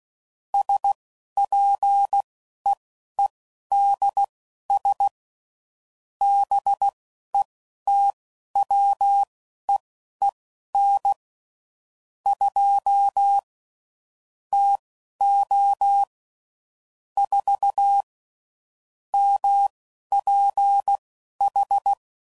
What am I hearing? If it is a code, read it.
Morse code, heard as a keyed tone, about 16 wpm, spaced as for 11 wpm, 800 Hz. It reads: SPEEDS BETWEEN 2 TO 4 MPH
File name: CW Code Groups @10WPM / Weather Sets / Weather 4 14